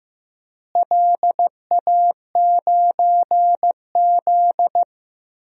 Morse code, heard as a keyed tone, about 15 wpm, 700 Hz. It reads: LA9Z
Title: DXCC-100 Call Signs - 4 Characters - Repeated 2x - 15wpm